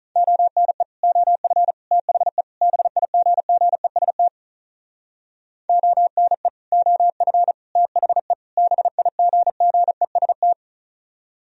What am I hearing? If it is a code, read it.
Morse code, heard as a keyed tone, about 43 wpm, 700 Hz. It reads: ONE OF THE BIGGEST ONE OF THE BIGGEST
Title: Speed-Racing - Sets of 4 Words 35wpm